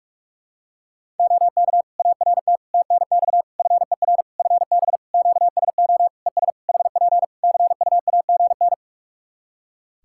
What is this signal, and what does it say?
Morse code, heard as a keyed tone, about 46 wpm, 700 Hz. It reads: OK ART TNX FER FB QSO ES HP CUAGN
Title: Practice QSO - Exchange 4 - Ending - 45wpm